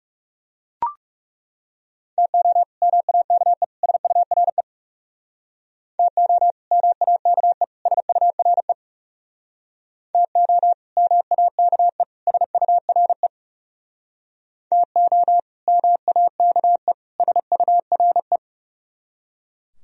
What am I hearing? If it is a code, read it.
Morse code, heard as a keyed tone, about 36 wpm, 700 Hz. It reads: TO MAKE SURE TO MAKE SURE TO MAKE SURE TO MAKE SURE
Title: Speed-Racing - Sets of 3 Words 30wpm